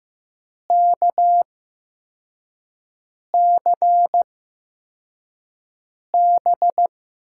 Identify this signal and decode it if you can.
Morse code, heard as a keyed tone, about 15 wpm, spaced as for 4 wpm, 700 Hz. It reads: KCB
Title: USA Call Signs - 6 Characters - 8x Character Spacing - 15wpm